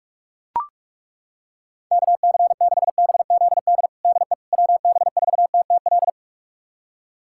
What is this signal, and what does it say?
Morse code, heard as a keyed tone, about 41 wpm, 700 Hz. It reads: KC6BZD DE WB4TTL
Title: USA Callsign DE Callsign - 6 Characters - 45wpm